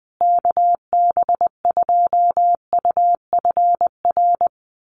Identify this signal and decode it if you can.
Morse code, heard as a keyed tone, about 20 wpm, 700 Hz. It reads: KB2UFR
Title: USA Call Signs Repeated 3x 20wpm